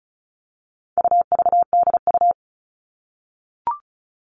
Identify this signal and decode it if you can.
Morse code, heard as a keyed tone, about 34 wpm, 700 Hz. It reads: UVDU
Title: All Cut Numbers - 4-Digit 35wpm